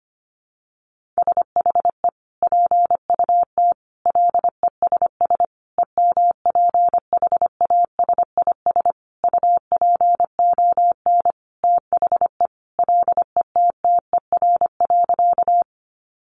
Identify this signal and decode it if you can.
Morse code, heard as a keyed tone, about 25 wpm, 700 Hz. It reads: SHE PUT LESS EMPHASIS UPON THE LETTER.
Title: Sentences from Top 1400 Words 25wpm